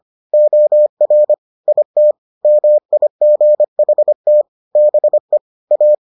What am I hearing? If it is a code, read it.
Morse code, heard as a keyed tone, about 25 wpm, 600 Hz. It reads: OR IT MIGHT BE A